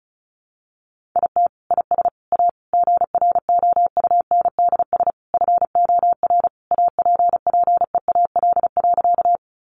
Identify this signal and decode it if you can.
Morse code, heard as a keyed tone, about 34 wpm, 700 Hz. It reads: IT IS A GROUNDS FOR APPEAL.
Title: Sentences from Top 1300 Words 35wpm